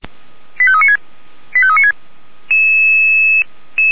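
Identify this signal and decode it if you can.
Five-tone selective call (ZVEI), heard as quick runs of five tones, tones of about 80 ms. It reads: 85396 85396
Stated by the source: ZVEI five-tone call 85371 received by SDR.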